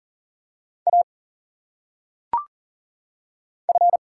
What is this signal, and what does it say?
Morse code, heard as a keyed tone, about 39 wpm, 700 Hz. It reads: A F
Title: Single Letters - Phonetic Pronunciation 40wpm